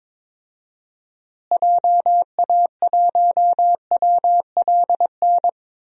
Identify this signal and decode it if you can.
Morse code, heard as a keyed tone, about 22 wpm, 700 Hz. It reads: JA1WLN
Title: DXCC-100 Call Signs - 6 Characters - Repeated 2x - 22wpm